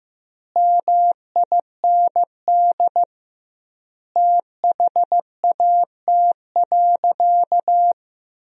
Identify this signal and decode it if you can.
Morse code, heard as a keyed tone, about 15 wpm, 700 Hz. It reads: MIND THAT.
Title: 2x Word-Spacing - Sentences from Top 500 Words - 15wpm